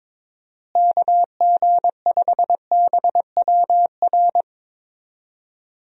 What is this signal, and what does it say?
Morse code, heard as a keyed tone, about 22 wpm, 700 Hz. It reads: KG5BWR
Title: USA Call Signs - 6 Characters - Repeated 3x - 22wpm